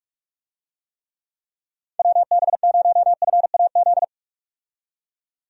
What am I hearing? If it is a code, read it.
Morse code, heard as a keyed tone, about 46 wpm, 700 Hz. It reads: WB0FAZ
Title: USA Call Signs - 6 Characters - 45wpm